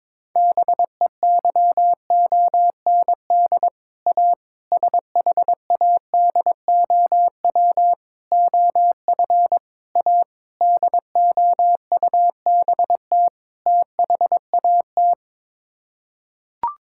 Morse code, heard as a keyed tone, about 22 wpm, 700 Hz. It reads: BEYOND A SHADOW OF A DOUBT THAT
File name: practice / 22wpm / Sets of 7 Words - 22wpm